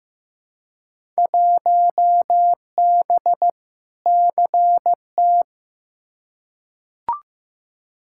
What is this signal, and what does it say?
Morse code, heard as a keyed tone, about 15 wpm, 700 Hz. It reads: 1B CT
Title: Field Day Exchanges Encore - 15wpm